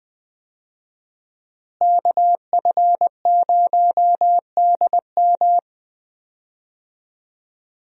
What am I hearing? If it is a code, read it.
Morse code, heard as a keyed tone, about 20 wpm, 700 Hz. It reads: KF0DM
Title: USA Call Signs - 5 Characters - Repeated 2x - 20wpm